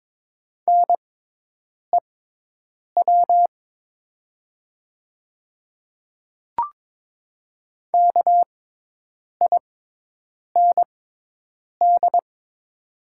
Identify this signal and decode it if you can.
Morse code, heard as a keyed tone, about 22 wpm, spaced as for 8 wpm, 700 Hz. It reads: NEW KIND
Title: 6x Character Spacing - Top 200 Words - 22wpm